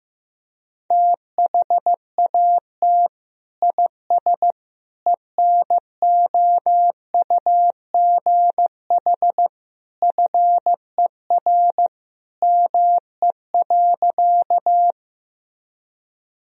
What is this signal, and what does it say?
Morse code, heard as a keyed tone, about 15 wpm, 700 Hz. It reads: THAT IS ENOUGH FER ME.
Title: Sentences from Top 400 Words and Abbreviations 15wpm